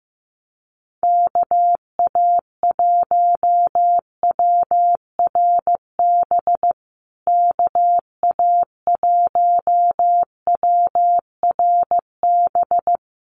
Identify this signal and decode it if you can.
Morse code, heard as a keyed tone, about 15 wpm, 700 Hz. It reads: KA1WRB KA1WRB
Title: USA Call Signs Repeated 3x 15wpm